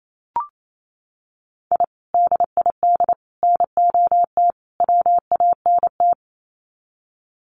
Morse code, heard as a keyed tone, about 28 wpm, 700 Hz. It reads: I DID NOT WANT
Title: Sets of 4 Words 28wpm